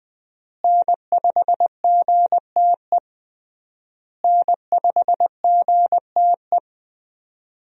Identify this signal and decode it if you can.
Morse code, heard as a keyed tone, about 21 wpm, 700 Hz. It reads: N5GTE N5GTE
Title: USA Call Signs - 5 Characters - Repeated 3x - 20wpm